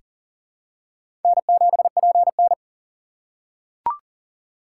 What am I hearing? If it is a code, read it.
Morse code, heard as a keyed tone, about 40 wpm, 700 Hz. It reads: N7PN
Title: USA Call Signs - 4 Characters - 6x Character Spacing - 40wpm